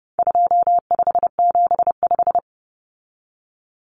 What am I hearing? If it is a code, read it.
Morse code, heard as a keyed tone, about 30 wpm, 700 Hz. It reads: 2575
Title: Numbers - 4-Digit 30wpm